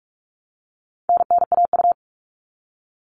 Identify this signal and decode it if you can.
Morse code, heard as a keyed tone, about 44 wpm, 700 Hz. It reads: NNAU
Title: All Cut Numbers - 4-Digit 45wpm